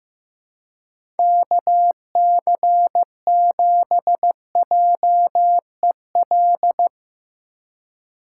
Morse code, heard as a keyed tone, about 15 wpm, 700 Hz. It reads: KC7JEL
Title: USA Call Signs - 3x Character Spacing - 15wpm